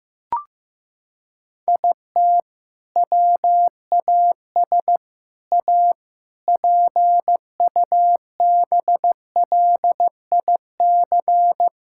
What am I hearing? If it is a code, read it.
Morse code, heard as a keyed tone, about 15 wpm, 700 Hz. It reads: IT WAS A PUBLIC
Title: Sentences from Top 1600 Words 15wpm